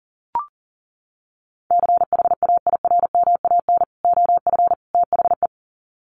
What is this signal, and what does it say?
Morse code, heard as a keyed tone, about 40 wpm, 700 Hz. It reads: CHAIRMAN OF THE
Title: Sets of 3 Words 40wpm